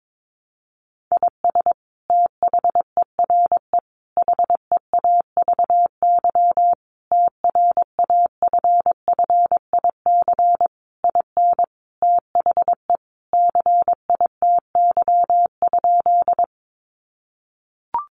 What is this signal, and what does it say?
Morse code, heard as a keyed tone, about 22 wpm, 700 Hz. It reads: IS THERE HEAVY TRAFFIC IN THE CITY?
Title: Sentences from Top 1300 Words 22wpm